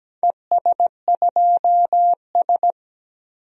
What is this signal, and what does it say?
Morse code, heard as a keyed tone, about 17 wpm, 700 Hz. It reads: ES2S
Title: DXCC-100 Call Signs - 4 Characters - 17wpm